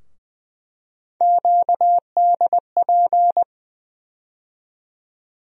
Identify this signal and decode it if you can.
Morse code, heard as a keyed tone, about 20 wpm, 700 Hz. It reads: QDP
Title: USA Call Sign Postfixes - 3 Letters 20wpm